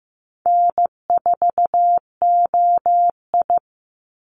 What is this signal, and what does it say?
Morse code, heard as a keyed tone, about 15 wpm, 700 Hz. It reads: N4OI
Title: Easy Call Signs Repeated 2x - 15wpm